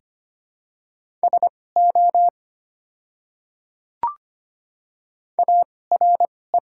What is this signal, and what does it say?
Morse code, heard as a keyed tone, about 25 wpm, spaced as for 18 wpm, 700 Hz. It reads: SO ARE
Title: 2x Character Spacing - Top 300 Words - 25wpm